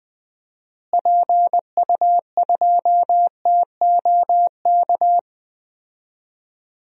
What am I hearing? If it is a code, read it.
Morse code, heard as a keyed tone, about 21 wpm, 700 Hz. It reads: PU2TOK
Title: DXCC-100 Call Signs - 6 Characters - Repeated 3x - 20wpm